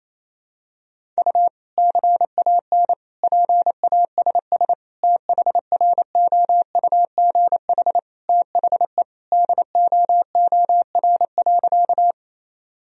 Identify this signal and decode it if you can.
Morse code, heard as a keyed tone, about 28 wpm, 700 Hz. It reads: U CAN PASS THROUGH THE DOOR.
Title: Sentences from Top 400 Words and Abbreviations 28wpm